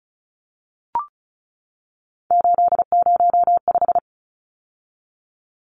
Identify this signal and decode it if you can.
Morse code, heard as a keyed tone, about 34 wpm, 700 Hz. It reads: 805
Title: Numbers - 3-Digit 35wpm